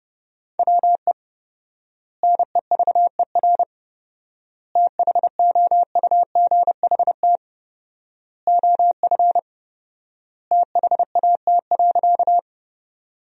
Morse code, heard as a keyed tone, about 31 wpm, 700 Hz. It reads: WE NEVER THOUGHT OF THAT.
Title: 4x Word-Spacing - Sentences from Top 300 Words - 30wpm